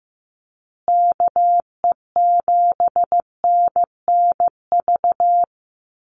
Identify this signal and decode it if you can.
Morse code, heard as a keyed tone, about 15 wpm, 700 Hz. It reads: KE7NNV
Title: USA Call Signs Repeated 2x 15wpm